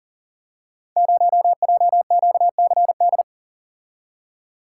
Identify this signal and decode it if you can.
Morse code, heard as a keyed tone, about 40 wpm, 700 Hz. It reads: 0JQCD
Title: Code Groups - 5 Letter:Numbers - 40wpm